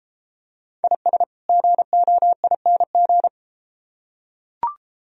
Morse code, heard as a keyed tone, about 33 wpm, 700 Hz. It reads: IS GOING